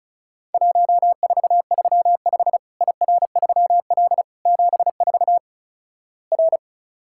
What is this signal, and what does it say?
Morse code, heard as a keyed tone, about 36 wpm, 700 Hz. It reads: 1435 IR3L 74 R
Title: Worked All Europe DX Contest - QTC Exchange - 35wpm